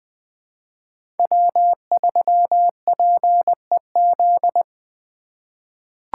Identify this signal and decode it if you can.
Morse code, heard as a keyed tone, about 20 wpm, 700 Hz. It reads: W3PEZ
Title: USA Call Signs - 5 Characters - 4x Character Spacing - 20wpm